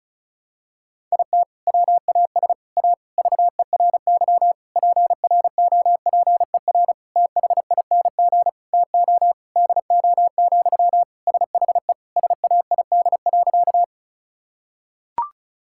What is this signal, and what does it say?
Morse code, heard as a keyed tone, about 36 wpm, 700 Hz. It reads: IT WAS A VERY PROPER THING TO DO, SHE SAID.
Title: Sentences from Top 1000 Words - Review Part 5 - 35wpm